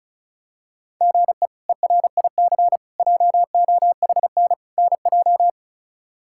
Morse code, heard as a keyed tone, about 34 wpm, 700 Hz. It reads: GE ERIC JOHN NJ
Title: K1USN SST QSO Practice - 35wpm